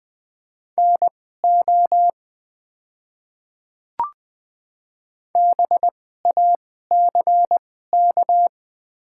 Morse code, heard as a keyed tone, about 20 wpm, spaced as for 14 wpm, 700 Hz. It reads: NO BACK
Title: 2x Character Spacing - Top 200 Words - 20wpm